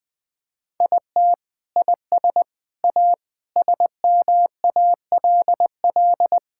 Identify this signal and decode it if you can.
Morse code, heard as a keyed tone, about 20 wpm, 700 Hz. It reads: IT IS A SMALL
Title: Sentences from Top 1000 Words - Review Part 5 - 20wpm